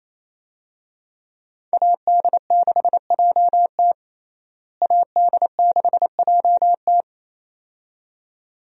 Morse code, heard as a keyed tone, about 28 wpm, 700 Hz. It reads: AD6JT AD6JT
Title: USA Call Signs - 5 Characters - Repeated 2x - 28wpm